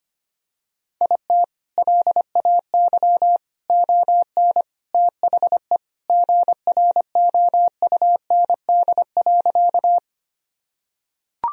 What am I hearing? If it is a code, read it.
Morse code, heard as a keyed tone, about 25 wpm, 700 Hz. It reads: IT LAY ON THE GROUND.
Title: Sentences from Top 1000 Words - Review Part 2 - 25wpm